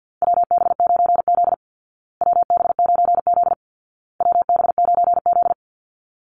Morse code, heard as a keyed tone, about 49 wpm, 700 Hz. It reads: WB9Z WB9Z WB9Z
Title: Call Signs Repeated 2x - 50wpm